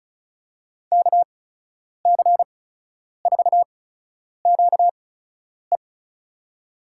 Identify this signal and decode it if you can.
Morse code, heard as a keyed tone, about 36 wpm, spaced as for 10 wpm, 700 Hz. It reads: KC4QE
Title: USA Call Signs - 5 Characters - 8x Character Spacing - 35wpm